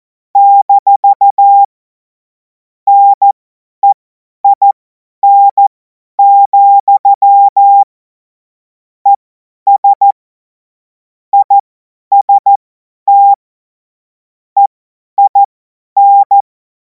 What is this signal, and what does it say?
Morse code, heard as a keyed tone, about 14 wpm, spaced as for 10 wpm, 800 Hz. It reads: - NEIN, ES IST EIN
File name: Morsekurs / Witze / Witz 09. Juni 2026 50 F70